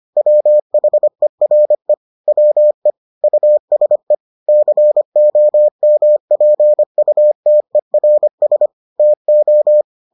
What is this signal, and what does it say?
Morse code, heard as a keyed tone, about 25 wpm, 600 Hz. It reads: WHERE WE USE COMPUTERS TO